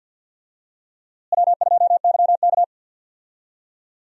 Morse code, heard as a keyed tone, about 46 wpm, 700 Hz. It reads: W2YK
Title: USA Call Signs - 4 Characters - Repeated 2x - 50wpm